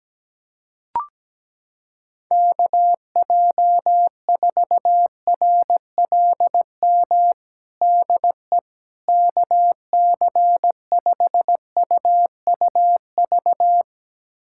Morse code, heard as a keyed tone, about 17 wpm, 700 Hz. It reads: KJ4RLM DE KC5UUV
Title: USA Callsign DE Callsign - 6 Characters - 17wpm